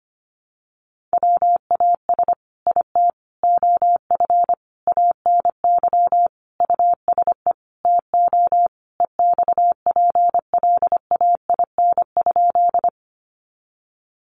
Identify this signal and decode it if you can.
Morse code, heard as a keyed tone, about 25 wpm, 700 Hz. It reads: WAS IT OF ANY USE TO EXPLAIN?
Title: Sentences from Top 1100 Words 25wpm